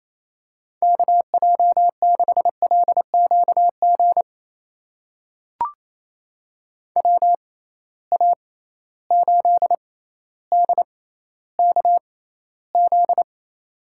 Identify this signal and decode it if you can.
Morse code, heard as a keyed tone, about 28 wpm, spaced as for 14 wpm, 700 Hz. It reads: KJ6LQG WA8DKZ
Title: USA Call Signs - 6 Characters - 6x Character Spacing - 28wpm